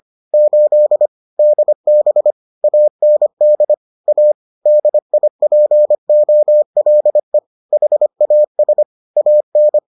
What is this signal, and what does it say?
Morse code, heard as a keyed tone, about 25 wpm, 600 Hz. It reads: 8 DB AND A DIPOLE HAS AN